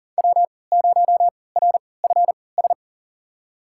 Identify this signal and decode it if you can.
Morse code, heard as a keyed tone, about 40 wpm, spaced as for 23 wpm, 700 Hz. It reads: W0RFS
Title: USA Call Signs - 3x Character Spacing - 40wpm